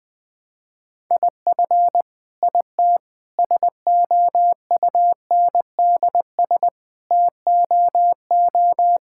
Morse code, heard as a keyed tone, about 21 wpm, 700 Hz. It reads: IF IT SOUNDS TOO
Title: Sentences with 9-Word Phrases - 20wpm